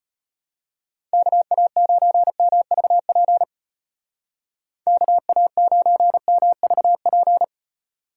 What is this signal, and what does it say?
Morse code, heard as a keyed tone, about 30 wpm, 700 Hz. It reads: KA9MVP KA9MVP
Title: Speed Racing - USA 2x3 Call Signs 25wpm